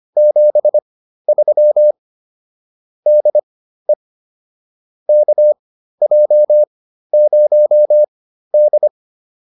Morse code, heard as a keyed tone, about 25 wpm, spaced as for 13 wpm, 600 Hz. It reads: 73 DE KJ0D